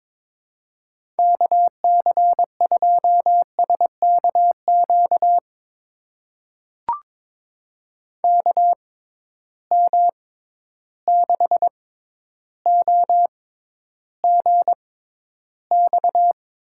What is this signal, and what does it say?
Morse code, heard as a keyed tone, about 22 wpm, spaced as for 11 wpm, 700 Hz. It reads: KC2SKQ KM6OGX
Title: USA Call Signs - 6 Characters - 6x Character Spacing - 22wpm